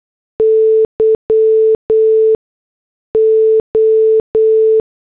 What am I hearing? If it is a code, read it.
Morse code, heard as a keyed tone, about 8 wpm, 440 Hz. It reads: Y O
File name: morse / headlines